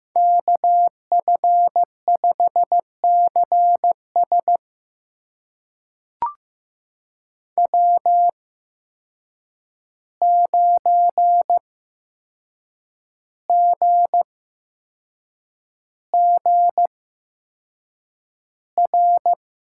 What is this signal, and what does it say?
Morse code, heard as a keyed tone, about 15 wpm, spaced as for 6 wpm, 700 Hz. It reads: KF5CS W9GGR
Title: USA Call Signs - 5 Characters - 8x Character Spacing - 15wpm